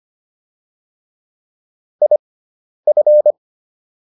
Morse code, heard as a keyed tone, about 25 wpm, spaced as for 10 wpm, 600 Hz. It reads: IF